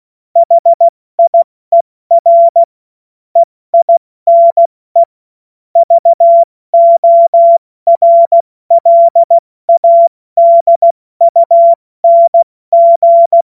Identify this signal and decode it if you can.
Morse code, heard as a keyed tone, about 16 wpm, 680 Hz. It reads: HIER EINE VORLADUNG
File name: Morsekurs / Witze / Witz 23. Dezember 2025 70 F80